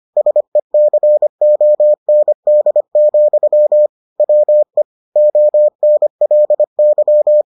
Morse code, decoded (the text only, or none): SECOND, WE ONLY